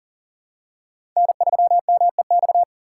Morse code, heard as a keyed tone, about 40 wpm, 700 Hz. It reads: N3MEX
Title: USA Call Signs - 4x Character Spacing - 40wpm